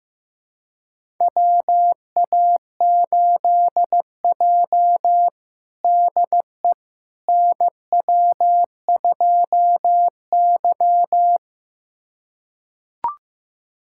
Morse code, heard as a keyed tone, about 16 wpm, 700 Hz. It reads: WA8J DE NW2Y
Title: USA Callsign DE Callsign - 4 Characters - 15wpm